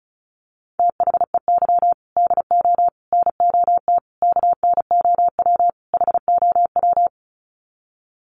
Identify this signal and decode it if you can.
Morse code, heard as a keyed tone, about 34 wpm, 700 Hz. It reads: THEY DO NOT KNOW HOW
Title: Sets of 5 Words Encore2 35wpm